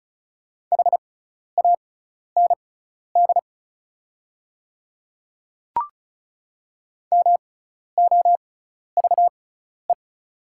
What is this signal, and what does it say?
Morse code, heard as a keyed tone, about 36 wpm, spaced as for 12 wpm, 700 Hz. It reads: HAND MOVE
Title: 6x Character Spacing - Top 200 Words - 35wpm